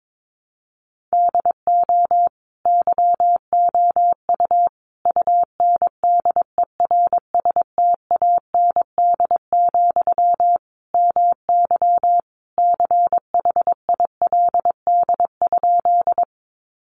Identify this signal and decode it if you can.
Morse code, heard as a keyed tone, about 22 wpm, 700 Hz. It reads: DO YOU UNDERSTAND, MY CHILD?
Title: Sentences from Top 800 Words 22wpm